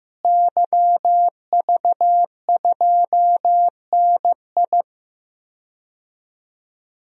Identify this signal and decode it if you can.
Morse code, heard as a keyed tone, about 15 wpm, 700 Hz. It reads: YV2NI
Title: DXCC-100 Call Signs - 5 Characters - Repeated 2x - 15wpm